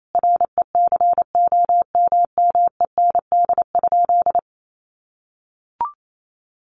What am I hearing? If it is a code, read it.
Morse code, heard as a keyed tone, about 28 wpm, 700 Hz. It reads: RECOMMEND?
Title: Sentences from Top 1200 Words 28wpm